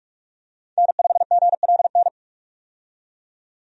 Morse code, heard as a keyed tone, about 42 wpm, 700 Hz. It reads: N5GLN
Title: USA Call Signs - 5 Characters - 45wpm